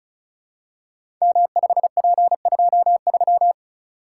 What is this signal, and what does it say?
Morse code, heard as a keyed tone, about 36 wpm, 690 Hz. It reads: M5P23
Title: Code Groups - 5 Letter:Numbers - 35wpm